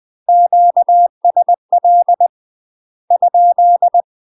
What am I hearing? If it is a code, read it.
Morse code, heard as a keyed tone, about 20 wpm, 700 Hz. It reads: QSL ?
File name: UK MASTER QSOs TRAINING / 20WPM / UK MASTER QSOs 0047